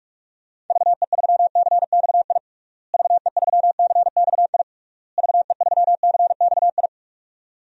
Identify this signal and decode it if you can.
Morse code, heard as a keyed tone, about 46 wpm, 700 Hz. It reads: VE3CXI VE3CXI VE3CXI
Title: DXCC-100 Call Signs - 6 Characters - Repeated 3x - 45wpm